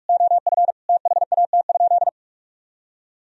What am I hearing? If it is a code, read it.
Morse code, heard as a keyed tone, about 46 wpm, 700 Hz. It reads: OF THAT?
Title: Sentences from Top 1000 Words - Review Part 2 - 45wpm